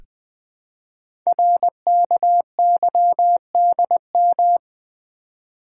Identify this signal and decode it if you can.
Morse code, heard as a keyed tone, about 20 wpm, 700 Hz. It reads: RKYDM
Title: Code Groups - 5 Letters - 20wpm